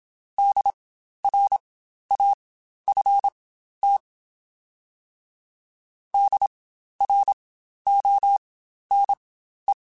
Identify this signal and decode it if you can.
Morse code, heard as a keyed tone, about 27 wpm, spaced as for 12 wpm, 800 Hz. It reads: DRAFT DRONE
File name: Spaced Alphas / Spaced Alphas mp3s / Ds sp @25